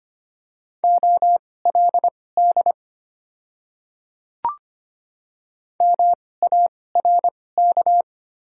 Morse code, heard as a keyed tone, about 25 wpm, spaced as for 18 wpm, 700 Hz. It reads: OLD MARK
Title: 2x Character Spacing - Top 300 Words - 25wpm